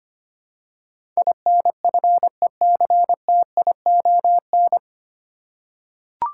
Morse code, heard as a keyed tone, about 25 wpm, 700 Hz. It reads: INFECTION
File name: Nine Letter Words - 25wpm